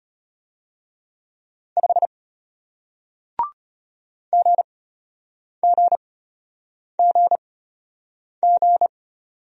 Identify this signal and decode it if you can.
Morse code, heard as a keyed tone, about 34 wpm, spaced as for 8 wpm, 700 Hz. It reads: 5 GGGG